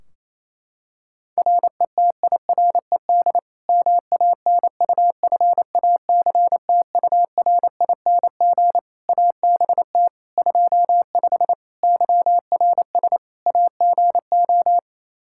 Morse code, heard as a keyed tone, about 27 wpm, 700 Hz. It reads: RETIRED MANUFACTURING ABT 25 YRS AGO
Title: K1USN SST QSO Practice 28wpm